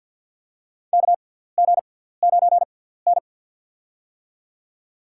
Morse code, heard as a keyed tone, about 52 wpm, spaced as for 17 wpm, 700 Hz. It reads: KG9N
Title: USA Call Signs - 4 Characters - 6x Character Spacing - 50wpm